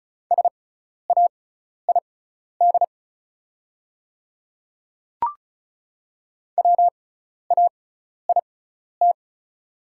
Morse code, heard as a keyed tone, about 34 wpm, spaced as for 12 wpm, 700 Hz. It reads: SAID WAIT
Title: 6x Character Spacing - Top 500 Words - 35wpm